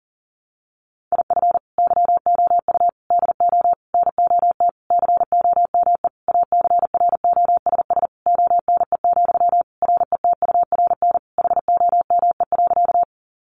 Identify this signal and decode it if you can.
Morse code, heard as a keyed tone, about 40 wpm, 700 Hz. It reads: IF YOU DO NOT COME ACROSS ONE, RETURN HOME.
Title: Sentences from Top 800 Words 40wpm